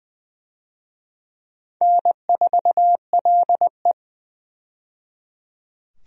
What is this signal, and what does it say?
Morse code, heard as a keyed tone, about 21 wpm, 700 Hz. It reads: N4LE
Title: USA Call Signs - 4 Characters - 20wpm